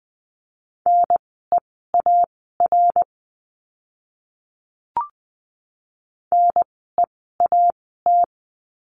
Morse code, heard as a keyed tone, about 20 wpm, spaced as for 14 wpm, 700 Hz. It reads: NEAR NEAT
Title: 054 Lesson 10f - Letter r - Words 20wpm